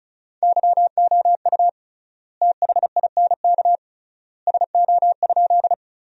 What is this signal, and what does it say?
Morse code, heard as a keyed tone, about 36 wpm, 700 Hz. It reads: YOU THINK SO?
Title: 3x Word-Spacing - Sentences from Top 200 Words - 35wpm